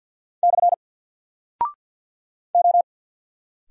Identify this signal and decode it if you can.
Morse code, heard as a keyed tone, about 51 wpm, 700 Hz. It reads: / O
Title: Single Character (Letter:Number:Punctuation:Prosign) 50wpm